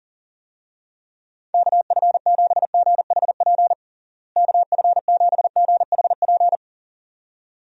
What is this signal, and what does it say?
Morse code, heard as a keyed tone, about 40 wpm, 700 Hz. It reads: KF7GHP KF7GHP
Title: USA Call Signs - 6 Characters - Repeated 2x - 40wpm